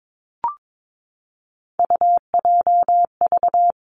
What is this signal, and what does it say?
Morse code, heard as a keyed tone, about 22 wpm, 700 Hz. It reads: UJV
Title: Three Challenging Letter-Number Combinations 22wpm